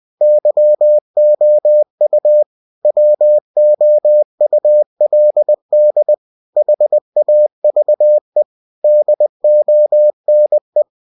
Morse code, heard as a keyed tone, about 20 wpm, 600 Hz. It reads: YOU WOULD HAVE DONE
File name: cw-books / winnie-the-pooh / winnie-20wpm-page-0107